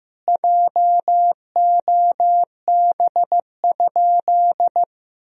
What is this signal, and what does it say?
Morse code, heard as a keyed tone, about 15 wpm, 700 Hz. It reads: JOB?
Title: Sentences from Top 1000 Words - Review Part 3 - 15wpm